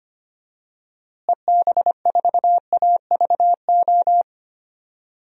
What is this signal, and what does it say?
Morse code, heard as a keyed tone, about 25 wpm, 690 Hz. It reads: EB4AVO
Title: DXCC-100 Call Signs - 6 Characters - Repeated 3x - 25wpm